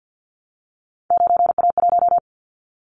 Morse code, heard as a keyed tone, about 50 wpm, 700 Hz. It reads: 9A1
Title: Callsign Prefixes for Top 25 DXCC - Repeated 3x - 50wpm